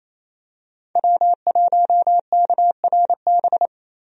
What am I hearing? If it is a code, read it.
Morse code, heard as a keyed tone, about 28 wpm, 700 Hz. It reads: W1KRB
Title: USA Call Signs - 5 Characters - Repeated 2x - 28wpm